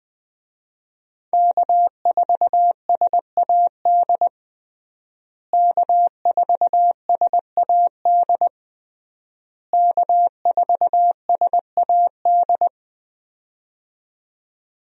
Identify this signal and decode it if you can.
Morse code, heard as a keyed tone, about 20 wpm, 700 Hz. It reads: K4SAD K4SAD K4SAD
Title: USA Call Signs - 5 Characters - Repeated 3x - 20wpm